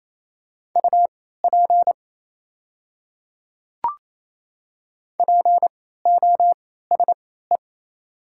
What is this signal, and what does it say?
Morse code, heard as a keyed tone, about 28 wpm, spaced as for 16 wpm, 690 Hz. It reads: UP POSE
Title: 3x Character Spacing - Top 500 Words - 28wpm